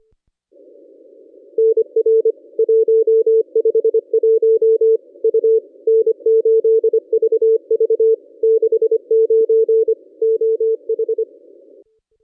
Morse code, heard as a keyed tone, about 26 wpm, 440 Hz. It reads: NR 151 U N8VV 69 OH
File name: contests / ss-cw-k5tr-op-n5rz -1100